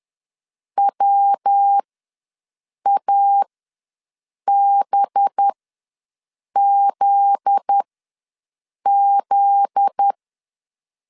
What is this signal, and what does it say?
Morse code, heard as a keyed tone, about 10 wpm, spaced as for 6 wpm, 800 Hz. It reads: WABZZ
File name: LEZIONI / 7 GF9Z / 7 GF9Z ws 10wpm 3919